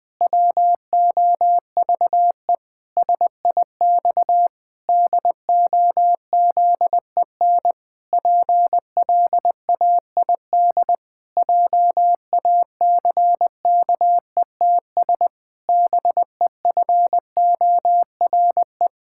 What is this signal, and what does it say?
Morse code, heard as a keyed tone, about 19 wpm, 700 Hz. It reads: WOVE SIX DOZEN PLAID JACKETS BEFORE
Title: Pangram Challenge v2 - 20wpm